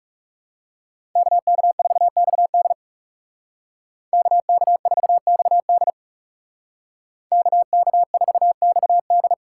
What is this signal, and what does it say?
Morse code, heard as a keyed tone, about 39 wpm, 700 Hz. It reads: KK4XD KK4XD KK4XD